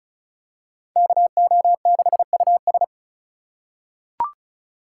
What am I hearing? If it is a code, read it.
Morse code, heard as a keyed tone, about 36 wpm, 700 Hz. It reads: KO6US
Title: USA Call Signs - 4x Character Spacing - 35wpm